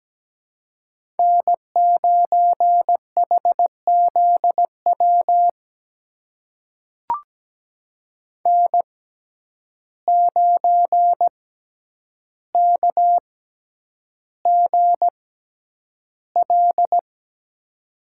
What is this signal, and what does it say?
Morse code, heard as a keyed tone, about 17 wpm, spaced as for 9 wpm, 700 Hz. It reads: N9HZW N9KGL
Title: USA Call Signs - 5 Characters - 6x Character Spacing - 17wpm